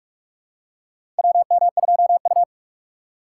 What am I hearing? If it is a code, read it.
Morse code, heard as a keyed tone, about 43 wpm, 700 Hz. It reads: WM2U
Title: USA Call Signs - 4 Characters - 3x Character Spacing - 45wpm